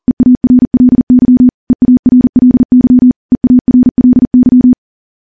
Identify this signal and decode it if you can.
Morse code, heard as a keyed tone, about 42 wpm, 250 Hz. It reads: EARLY EARLY EARLY
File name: morse / morse-training-40WPM